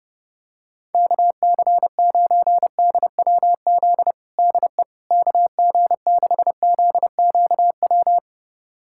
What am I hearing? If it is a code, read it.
Morse code, heard as a keyed tone, about 30 wpm, 700 Hz. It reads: KC9DWZ DE KG6ZQW
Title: USA Callsign DE Callsign - 6 Characters - 30wpm